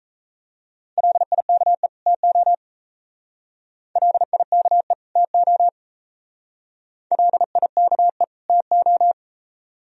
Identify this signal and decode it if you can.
Morse code, heard as a keyed tone, about 35 wpm, 700 Hz. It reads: LIKE TO LIKE TO LIKE TO
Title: Speed-Racing - Sets of 2 Words 28wpm